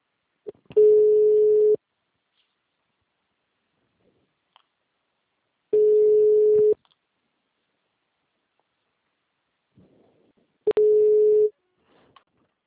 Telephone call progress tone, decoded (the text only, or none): ringback tone